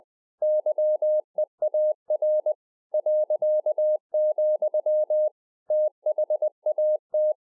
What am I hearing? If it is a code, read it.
Morse code, heard as a keyed tone, about 20 wpm, 610 Hz. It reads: YEAR ., THAT
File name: 20WPM